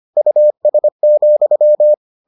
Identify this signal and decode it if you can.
Morse code, heard as a keyed tone, about 25 wpm, 600 Hz. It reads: US,